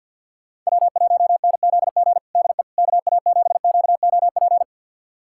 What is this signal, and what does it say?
Morse code, heard as a keyed tone, about 51 wpm, 690 Hz. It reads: W1NZG DE KA7QOP
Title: USA Callsign DE Callsign 50wpm